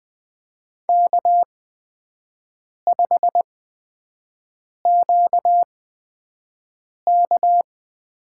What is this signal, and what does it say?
Morse code, heard as a keyed tone, about 20 wpm, spaced as for 5 wpm, 700 Hz. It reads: K5QK
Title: USA Call Signs - 4 Characters - 8x Character Spacing - 20wpm